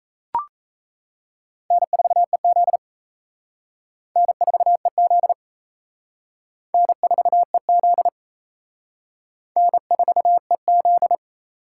Morse code, heard as a keyed tone, about 37 wpm, 700 Hz. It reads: N4EZ N4EZ N4EZ N4EZ